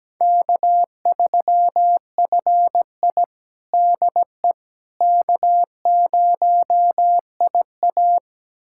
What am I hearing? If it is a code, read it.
Morse code, heard as a keyed tone, about 17 wpm, 700 Hz. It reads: K3FI DE K0IA
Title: USA Callsign DE Callsign - 4 Characters - 17wpm